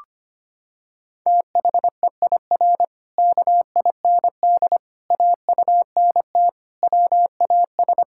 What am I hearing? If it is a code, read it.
Morse code, heard as a keyed tone, about 25 wpm, 700 Hz. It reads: THEIR KIND AUNT WAS
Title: Pangram Challenge v2 - 25wpm